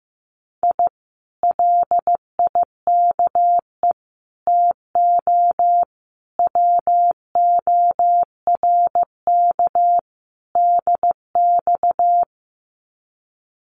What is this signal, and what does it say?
Morse code, heard as a keyed tone, about 15 wpm, 700 Hz. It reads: I LIKE TO WORK DX
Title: Common QSO Phrases Encore 15wpm